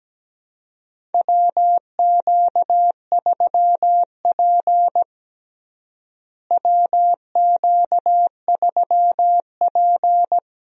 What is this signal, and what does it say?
Morse code, heard as a keyed tone, about 17 wpm, 700 Hz. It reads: WQ3P WQ3P
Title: USA Call Signs - 4 Characters - Repeated 3x - 17wpm